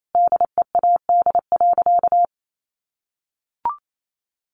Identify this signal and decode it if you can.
Morse code, heard as a keyed tone, about 28 wpm, 700 Hz. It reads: DEAD.
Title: Sentences from Top 1500 Words 28wpm